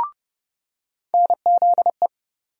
Morse code, heard as a keyed tone, about 30 wpm, 700 Hz. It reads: NZE
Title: USA Call Sign Postfixes - 3 Letters 30wpm